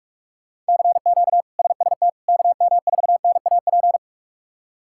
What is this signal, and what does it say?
Morse code, heard as a keyed tone, about 43 wpm, 700 Hz. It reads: CQ SST KM4NAP
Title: K1USN SST QSO Practice - 45wpm